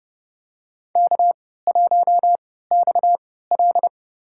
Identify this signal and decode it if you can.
Morse code, heard as a keyed tone, about 30 wpm, spaced as for 17 wpm, 700 Hz. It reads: K1XL
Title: USA Call Signs - 4 Characters - 3x Character Spacing - 30wpm